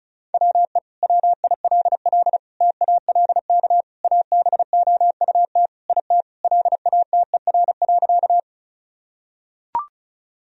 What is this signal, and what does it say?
Morse code, heard as a keyed tone, about 34 wpm, 700 Hz. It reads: WE WILL TALK ABOUT IT LATER.
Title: Sentences from Top 1000 Words - Review Part 4 - 35wpm